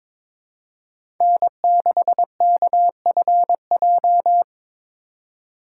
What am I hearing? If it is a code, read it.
Morse code, heard as a keyed tone, about 22 wpm, 700 Hz. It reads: N6KFJ
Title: USA Call Signs - 5 Characters - 3x Character Spacing - 22wpm